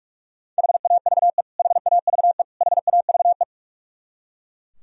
Morse code, heard as a keyed tone, about 41 wpm, 700 Hz. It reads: HAVE HAVE HAVE
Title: Top 100 Words Repeated 45wpm